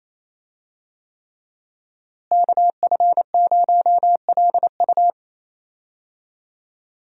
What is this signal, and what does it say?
Morse code, heard as a keyed tone, about 28 wpm, 700 Hz. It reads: KF0LU
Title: USA Call Signs - 5 Characters - Repeated 3x - 28wpm